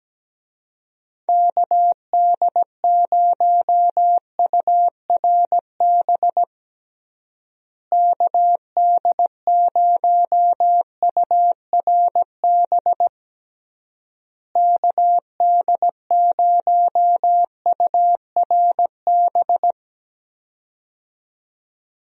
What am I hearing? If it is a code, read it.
Morse code, heard as a keyed tone, about 17 wpm, 700 Hz. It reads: KD0URB KD0URB KD0URB
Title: USA Call Signs - 6 Characters - Repeated 3x - 17wpm